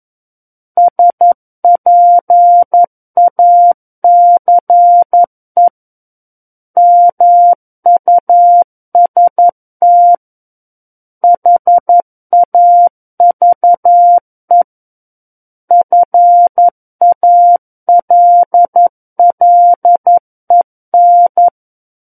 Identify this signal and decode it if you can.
Morse code, heard as a keyed tone, about 11 wpm, 700 Hz. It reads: SPACE MUST HAVE FALLEN
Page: SKCC - War of the Worlds - 02-Chapter 2 - 11 WPM